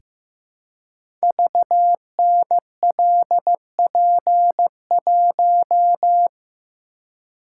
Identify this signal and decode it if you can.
Morse code, heard as a keyed tone, about 15 wpm, 700 Hz. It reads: VNLP1
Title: Code Groups - 5 Letter:Numbers - 15wpm